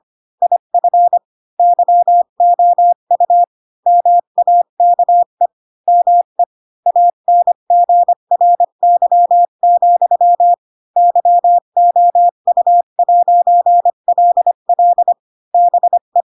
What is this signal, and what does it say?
Morse code, heard as a keyed tone, about 25 wpm, 700 Hz. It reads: IF YOU MAKE ME ANGRY, YOU'LL BE
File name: Pinocchio / Pinocchio ch4 20 25 ik1hge